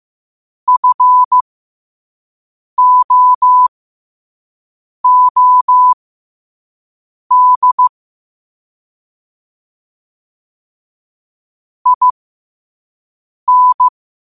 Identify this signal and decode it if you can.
Morse code, heard as a keyed tone, about 15 wpm, spaced as for 5 wpm, 1000 Hz. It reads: FOOD IN
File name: qotdfiles / 5wpm